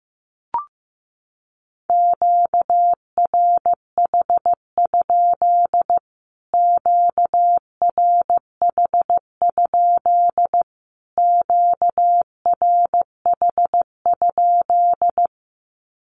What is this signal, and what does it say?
Morse code, heard as a keyed tone, about 15 wpm, 700 Hz. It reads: QRH? QRH? QRH?
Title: Q-Codes Repeated 3x 15wpm